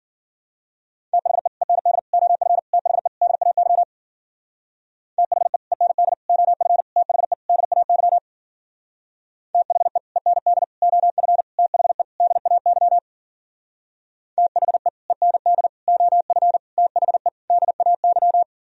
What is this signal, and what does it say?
Morse code, heard as a keyed tone, about 52 wpm, 700 Hz. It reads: THE END OF THE DAY THE END OF THE DAY THE END OF THE DAY THE END OF THE DAY